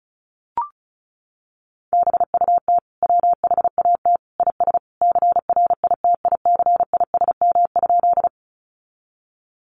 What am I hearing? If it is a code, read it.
Morse code, heard as a keyed tone, about 34 wpm, 700 Hz. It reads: BUT WHAT IS CRITICISM?
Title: Sentences from Top 1300 Words 35wpm